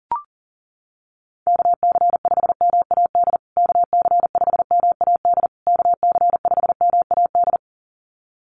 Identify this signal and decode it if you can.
Morse code, heard as a keyed tone, about 40 wpm, 700 Hz. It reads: KC5MAD KC5MAD KC5MAD
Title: USA Call Signs Repeated 3x 40wpm